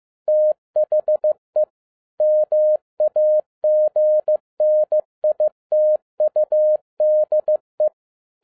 Morse code, heard as a keyed tone, about 15 wpm, 610 Hz. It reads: THE MAGNITUDE
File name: 2011-10-mp3 / n02-FIELD DAY CONTEST QRV FROM SAKAE VILLAGE 15wpm